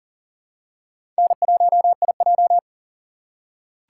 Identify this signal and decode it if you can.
Morse code, heard as a keyed tone, about 40 wpm, 700 Hz. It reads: N1IJ
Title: USA Call Signs - 4x Character Spacing - 40wpm